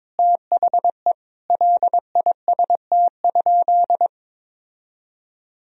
Morse code, heard as a keyed tone, about 22 wpm, 700 Hz. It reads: THE LIST?
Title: Sentences from Top 1000 Words - Review Part 2 - 22wpm